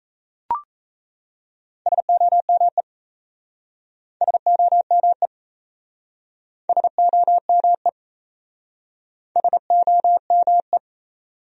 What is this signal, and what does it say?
Morse code, heard as a keyed tone, about 37 wpm, 700 Hz. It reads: SOME SOME SOME SOME